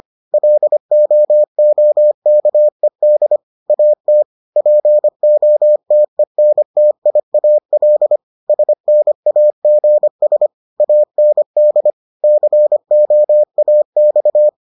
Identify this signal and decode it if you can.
Morse code, heard as a keyed tone, about 25 wpm, 600 Hz. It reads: LOOKED AT POTENTIAL SNAGS AND COAX